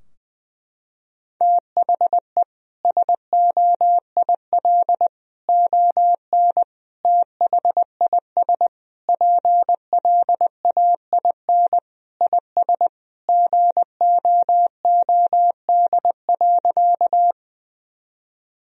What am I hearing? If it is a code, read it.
Morse code, heard as a keyed tone, about 20 wpm, 700 Hz. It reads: THE SOIL ON THIS PLAIN IS GOOD.
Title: Sentences from Top 1000 Words - Review Part 4 - 20wpm